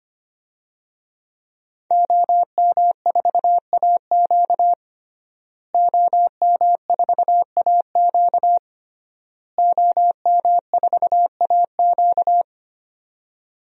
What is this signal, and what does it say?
Morse code, heard as a keyed tone, about 25 wpm, 700 Hz. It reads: OM4AQ OM4AQ OM4AQ
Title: DXCC-100 Call Signs - 5 Characters - Repeated 3x - 25wpm